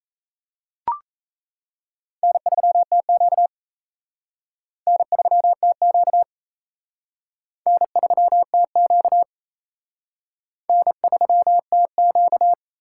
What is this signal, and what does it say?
Morse code, heard as a keyed tone, about 37 wpm, 700 Hz. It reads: N3TQ N3TQ N3TQ N3TQ